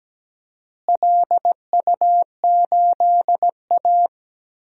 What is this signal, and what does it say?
Morse code, heard as a keyed tone, about 17 wpm, 700 Hz. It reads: LU8A
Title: DXCC-100 Call Signs - 4 Characters - Repeated 3x - 17wpm